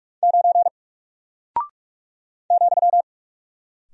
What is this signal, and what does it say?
Morse code, heard as a keyed tone, about 44 wpm, 700 Hz. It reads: 9 ,
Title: Single Character (Letter:Number:Punctuation:Prosign) Alt 45wpm